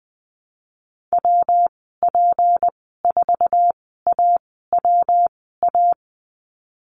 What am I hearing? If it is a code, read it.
Morse code, heard as a keyed tone, about 20 wpm, spaced as for 15 wpm, 700 Hz. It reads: WP4AWA
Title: 161 Lesson 26g - Letter g - Call Signs 20wpm